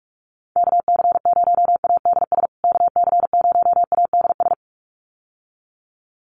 Morse code, heard as a keyed tone, about 44 wpm, 700 Hz. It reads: KC0ADS KC0ADS
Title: USA Call Signs Repeated 2x 45wpm